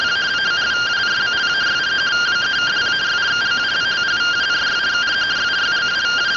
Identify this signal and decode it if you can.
Two-tone teleprinter signal, about 170 Hz shift.